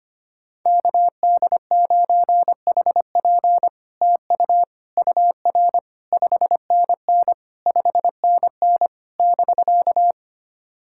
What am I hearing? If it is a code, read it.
Morse code, heard as a keyed tone, about 25 wpm, 700 Hz. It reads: KD9HP TU UR 5NN 5NN <BK>
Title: SOTA QSO Practice 25wpm